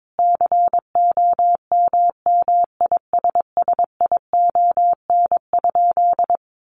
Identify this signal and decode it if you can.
Morse code, heard as a keyed tone, about 22 wpm, 700 Hz. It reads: COMMISSION?
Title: Sentences from Top 1400 Words 22wpm